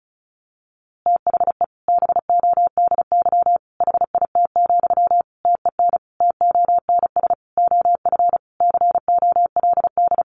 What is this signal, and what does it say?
Morse code, heard as a keyed tone, about 34 wpm, 700 Hz. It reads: THE BODY HIT, TEN TONS OF COLD
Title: A Sound Of Thunder 35wpm